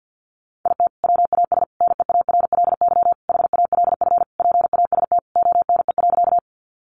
Morse code, heard as a keyed tone, about 50 wpm, 700 Hz. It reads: IT WAS NEARLY HALF PAST ONE.
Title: Sentences from Top 1300 Words 50wpm